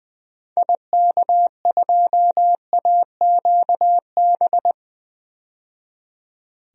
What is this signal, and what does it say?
Morse code, heard as a keyed tone, about 20 wpm, 700 Hz. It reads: IK2AQB
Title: DXCC-100 Call Signs - 6 Characters - 20wpm